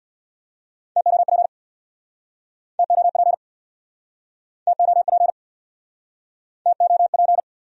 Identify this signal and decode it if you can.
Morse code, heard as a keyed tone, about 68 wpm, 700 Hz. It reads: TOP TOP TOP TOP